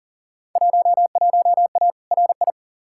Morse code, heard as a keyed tone, about 40 wpm, 700 Hz. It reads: 11A RI
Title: Field Day Exchanges Encore - 40wpm